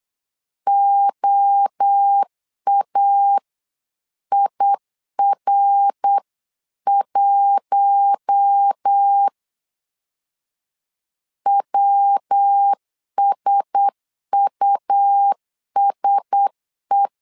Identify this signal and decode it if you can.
Morse code, heard as a keyed tone, about 8 wpm, 800 Hz. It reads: OAIR1 WSUSE
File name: LEZIONI / 4 Q1S 8wpm 2710